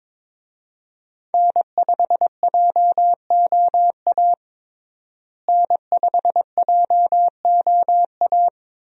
Morse code, heard as a keyed tone, about 21 wpm, 700 Hz. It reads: N5JOA N5JOA
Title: USA Call Signs - 5 Characters - Repeated 3x - 22wpm